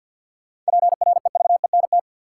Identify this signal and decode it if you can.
Morse code, heard as a keyed tone, about 53 wpm, 700 Hz. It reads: PREVENT
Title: Sentences from Top 1100 Words - New Word Drill - 50wpm